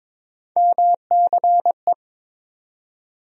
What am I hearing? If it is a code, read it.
Morse code, heard as a keyed tone, about 23 wpm, 700 Hz. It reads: MCE
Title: USA Call Sign Postfixes - 3 Letters 22wpm